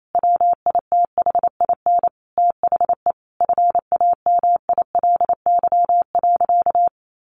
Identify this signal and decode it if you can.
Morse code, heard as a keyed tone, about 28 wpm, 700 Hz. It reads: WITHIN THE FAMILY.
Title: Sentences from Top 1500 Words 28wpm